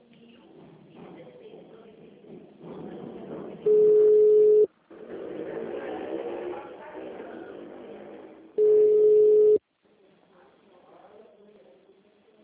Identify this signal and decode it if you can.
Ringback tone (Europe, 425 Hz).